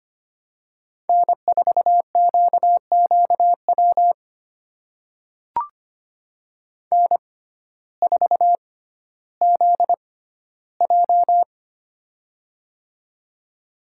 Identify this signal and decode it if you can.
Morse code, heard as a keyed tone, about 25 wpm, spaced as for 14 wpm, 700 Hz. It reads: N4QQW N4ZJ
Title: USA Call Signs - 6x Character Spacing - 25wpm